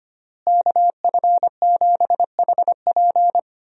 Morse code, heard as a keyed tone, about 25 wpm, 700 Hz. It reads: KF7HP
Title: USA Call Signs - 5 Characters - Repeated 3x - 25wpm